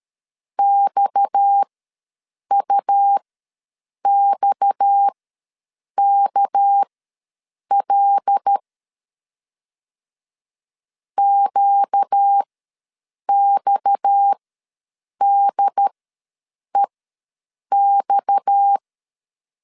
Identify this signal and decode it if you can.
Morse code, heard as a keyed tone, about 13 wpm, spaced as for 7 wpm, 800 Hz. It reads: XUXKL QXDEX
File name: LEZIONI / 6 PKX0 / 6 PKX ws 12wpm 3106